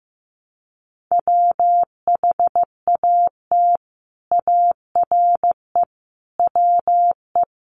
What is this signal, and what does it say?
Morse code, heard as a keyed tone, about 15 wpm, 700 Hz. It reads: WHAT ARE WE
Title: Sentences from Top 100 Words 15wpm